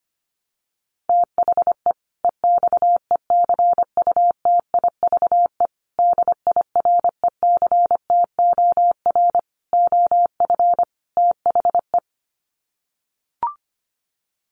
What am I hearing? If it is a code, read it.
Morse code, heard as a keyed tone, about 25 wpm, 700 Hz. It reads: THE EXECUTIVE DIRECTOR OF THE
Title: Sets of 5 Words Encore 25wpm